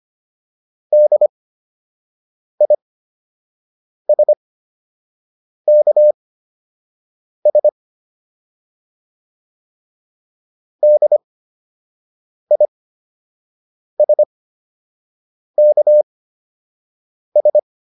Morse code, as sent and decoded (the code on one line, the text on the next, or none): -.. .. ... -.- ... / -.. .. ... -.- ...
DISKS DISKS